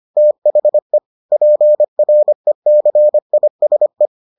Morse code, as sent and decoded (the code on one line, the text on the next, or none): - .... / .--. .-. -.-. .. ... .
THE PRECISE